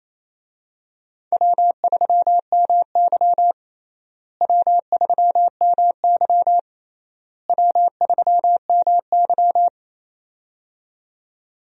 Morse code, heard as a keyed tone, about 28 wpm, 700 Hz. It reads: W3MY W3MY W3MY
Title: USA Call Signs - 4 Characters - Repeated 3x - 28wpm